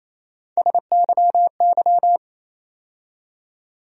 Morse code, heard as a keyed tone, about 28 wpm, 700 Hz. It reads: SYY
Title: USA Call Sign Postfixes - 3 Letters 28wpm